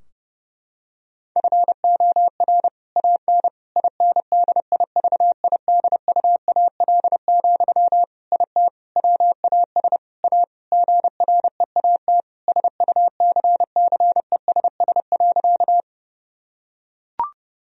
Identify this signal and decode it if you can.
Morse code, heard as a keyed tone, about 30 wpm, 700 Hz. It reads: FOR AN INDIVIDUAL, IT WAS A GREAT SUCCESS.
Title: Sentences from Top 1000 Words - Review Part 4 - 30wpm